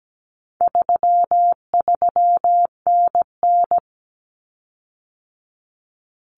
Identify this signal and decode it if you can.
Morse code, heard as a keyed tone, about 17 wpm, 700 Hz. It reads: 33NN
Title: Common Cut Numbers - 4-Digit 17wpm